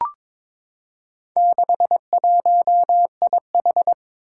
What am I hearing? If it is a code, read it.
Morse code, heard as a keyed tone, about 23 wpm, 700 Hz. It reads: 61IH
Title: Four Challenging Letter-Number Combinations 22wpm